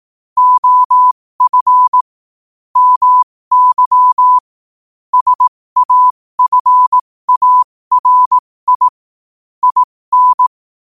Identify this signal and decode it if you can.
Morse code, heard as a keyed tone, about 18 wpm, spaced as for 15 wpm, 1000 Hz. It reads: OF MY SAFARI IN
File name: qotdfiles / 15wpm